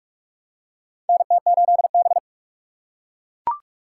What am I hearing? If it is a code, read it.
Morse code, heard as a keyed tone, about 44 wpm, 700 Hz. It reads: NT8B
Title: USA Call Signs - 4 Characters - 2x Character Spacing - 45wpm